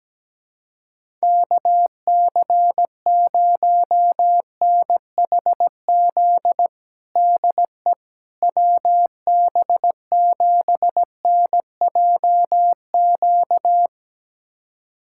Morse code, heard as a keyed tone, about 17 wpm, 700 Hz. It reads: KC0NHZ DE WB7NJQ
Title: USA Callsign DE Callsign - 6 Characters - 17wpm